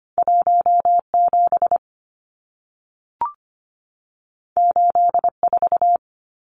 Morse code, heard as a keyed tone, about 25 wpm, 700 Hz. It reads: 17 84
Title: Numbers - 2-Digit 25wpm